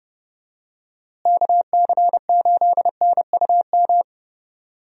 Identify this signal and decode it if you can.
Morse code, heard as a keyed tone, about 30 wpm, 700 Hz. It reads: KC8NUM
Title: USA Call Signs - 6 Characters - Repeated 3x - 30wpm